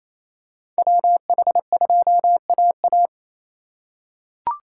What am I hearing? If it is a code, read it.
Morse code, heard as a keyed tone, about 28 wpm, 700 Hz. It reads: WH2AA
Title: USA Call Signs - 5 Characters - 2x Character Spacing - 28wpm